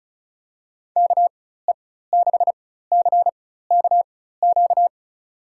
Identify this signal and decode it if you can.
Morse code, heard as a keyed tone, about 36 wpm, spaced as for 16 wpm, 700 Hz. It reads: KE6CKQ
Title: USA Call Signs - 6 Characters - 4x Character Spacing - 35wpm